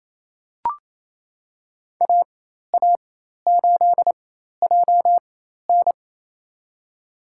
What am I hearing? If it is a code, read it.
Morse code, heard as a keyed tone, about 27 wpm, spaced as for 13 wpm, 700 Hz. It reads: AA8JN
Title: USA Call Signs - 5 Characters - 4x Character Spacing - 28wpm